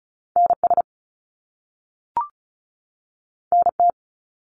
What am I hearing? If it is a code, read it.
Morse code, heard as a keyed tone, about 36 wpm, 700 Hz. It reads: NS NT
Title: Canadian Provinces and Territories 35wpm